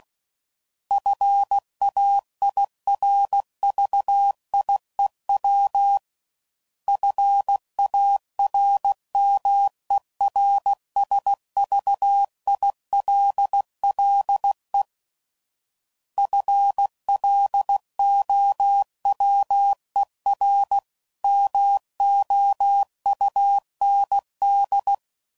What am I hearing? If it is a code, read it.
Morse code, heard as a keyed tone, about 16 wpm, 800 Hz. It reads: FAIRVIEW FARMERSVILLE FLOWER MOUND
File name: CW Code Groups @15WPM